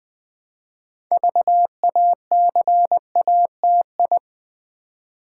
Morse code, heard as a keyed tone, about 20 wpm, 700 Hz. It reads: VACATI
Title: Build A Word - 3 Letters At A Time - 8-Letter Words - 20wpm